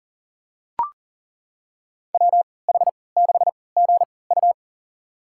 Morse code, heard as a keyed tone, about 40 wpm, spaced as for 23 wpm, 700 Hz. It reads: WH6GU
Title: USA Call Signs - 3x Character Spacing - 40wpm